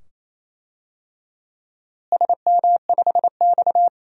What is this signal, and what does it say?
Morse code, heard as a keyed tone, about 27 wpm, 700 Hz. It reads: SM5X
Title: DXCC-100 Call Signs - 4 Characters - Repeated 2x - 28wpm